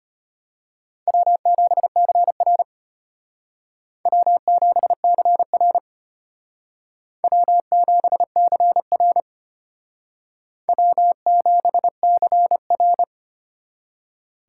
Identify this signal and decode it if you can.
Morse code, heard as a keyed tone, about 33 wpm, 700 Hz. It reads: W7CR W7CR W7CR W7CR